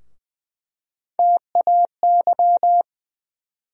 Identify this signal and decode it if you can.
Morse code, heard as a keyed tone, about 20 wpm, 700 Hz. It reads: TAY
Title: USA Call Sign Postfixes - 3 Letters 20wpm